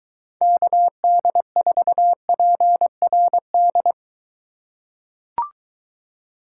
Morse code, heard as a keyed tone, about 23 wpm, 700 Hz. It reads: KD4PRD